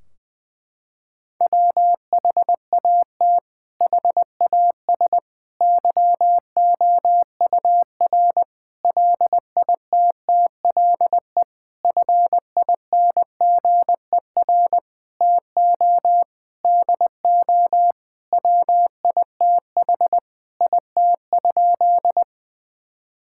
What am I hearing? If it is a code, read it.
Morse code, heard as a keyed tone, about 20 wpm, 700 Hz. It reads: WHAT HAS YOUR LITTLE FINGER TO DO WITH IT?
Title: Sentences from Top 1600 Words 20wpm